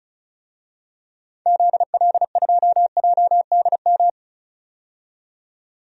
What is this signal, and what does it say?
Morse code, heard as a keyed tone, about 36 wpm, 690 Hz. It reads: ZL2JDM
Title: DXCC-100 Call Signs - 6 Characters - 35wpm